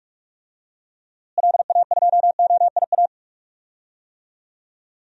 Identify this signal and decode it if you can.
Morse code, heard as a keyed tone, about 46 wpm, 700 Hz. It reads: LA2OIA
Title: DXCC-100 Call Signs - 6 Characters - Repeated 2x - 45wpm